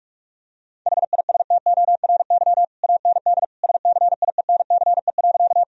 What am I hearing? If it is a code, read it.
Morse code, heard as a keyed tone, about 46 wpm, 700 Hz. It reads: HISTORY AND SCIENCE.
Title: Sentences from Top 1000 Words - Review Part 2 - 45wpm